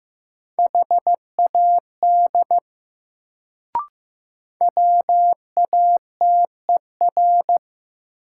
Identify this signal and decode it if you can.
Morse code, heard as a keyed tone, about 15 wpm, 700 Hz. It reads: HAD WATER
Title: Sentences from Top 100 Words - New Word Drill - 15wpm